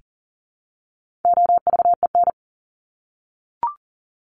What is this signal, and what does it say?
Morse code, heard as a keyed tone, about 40 wpm, 700 Hz. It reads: OVEN
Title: Top 501-750 Words 40wpm